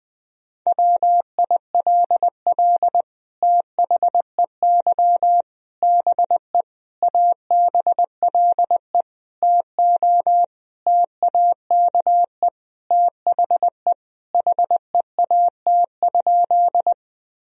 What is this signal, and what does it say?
Morse code, heard as a keyed tone, about 21 wpm, 700 Hz. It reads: WILL THEY BE ABLE TO TAKE THE HEAT?
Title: Sentences from Top 1000 Words - Review Part 3 - 20wpm